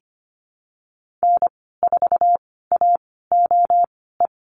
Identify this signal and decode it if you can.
Morse code, heard as a keyed tone, about 25 wpm, spaced as for 16 wpm, 700 Hz. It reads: N4AOE
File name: Lesson 15g - Number 5 - Call Signs 25wpm